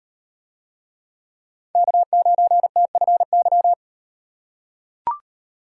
Morse code, heard as a keyed tone, about 38 wpm, 700 Hz. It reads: K9TFY